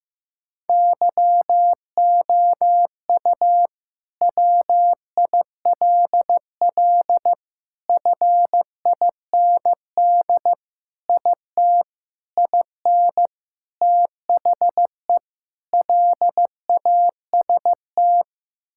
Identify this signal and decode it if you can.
Morse code, heard as a keyed tone, about 16 wpm, 700 Hz. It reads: YOU WILL FIND IT IN THE LAST
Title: Sentences from Top 1000 Words - Review Part 5 - 15wpm